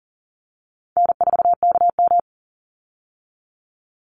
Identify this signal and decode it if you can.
Morse code, heard as a keyed tone, about 40 wpm, 700 Hz. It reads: N4KM
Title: Easy Call Signs Repeated 2x - 40wpm